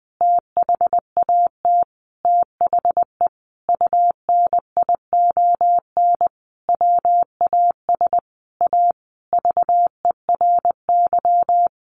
Morse code, heard as a keyed tone, about 20 wpm, 700 Hz. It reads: THAT THE UNION WAS A VERY
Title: Sentences from Top 1400 Words 20wpm